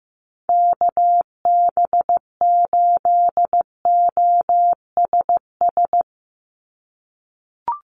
Morse code, heard as a keyed tone, about 15 wpm, 700 Hz. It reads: KB8OSS
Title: USA Call Signs 15wpm